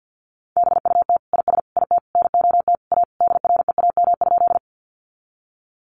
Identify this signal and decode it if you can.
Morse code, heard as a keyed tone, about 50 wpm, 700 Hz. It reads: BUT IS IT NOT A DREAM?
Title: Sentences from Top 1500 Words 50wpm